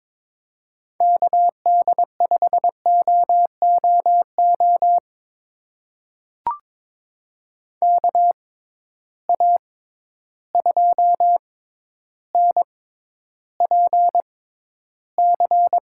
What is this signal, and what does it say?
Morse code, heard as a keyed tone, about 22 wpm, spaced as for 11 wpm, 700 Hz. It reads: KD5OOO KA2NPC
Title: USA Call Signs - 6 Characters - 6x Character Spacing - 22wpm